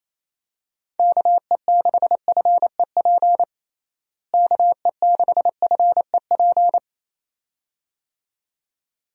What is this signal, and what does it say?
Morse code, heard as a keyed tone, about 28 wpm, 700 Hz. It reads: KE6FEP KE6FEP
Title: USA Call Signs - 6 Characters - Repeated 2x - 28wpm